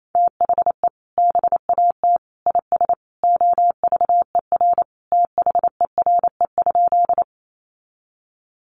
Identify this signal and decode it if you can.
Morse code, heard as a keyed tone, about 28 wpm, 700 Hz. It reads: THE BAT IS OVER THERE?
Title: Sentences from Top 1500 Words 28wpm